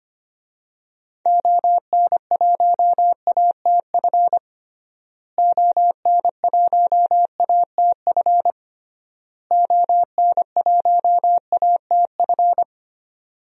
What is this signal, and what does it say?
Morse code, heard as a keyed tone, about 25 wpm, 700 Hz. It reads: ON1ATF ON1ATF ON1ATF
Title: DXCC-100 Call Signs - 6 Characters - Repeated 3x - 25wpm